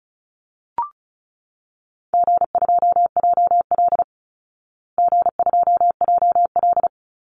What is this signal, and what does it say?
Morse code, heard as a keyed tone, about 34 wpm, 700 Hz. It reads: G2JL G2JL
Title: Easy Call Signs Repeated 2x - 35wpm